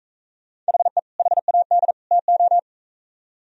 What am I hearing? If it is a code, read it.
Morse code, heard as a keyed tone, about 42 wpm, 700 Hz. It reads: HE HAD TO